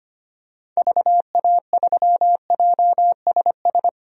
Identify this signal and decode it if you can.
Morse code, heard as a keyed tone, about 25 wpm, 690 Hz. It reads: VA3JSS
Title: DXCC-100 Call Signs - 6 Characters - 25wpm